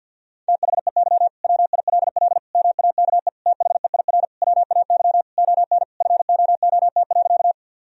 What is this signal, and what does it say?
Morse code, heard as a keyed tone, about 50 wpm, 700 Hz. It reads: THEY WILL MAKE THEIR WAY ON FOOT.
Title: Sentences from Top 1000 Words - Review Part 3 - 50wpm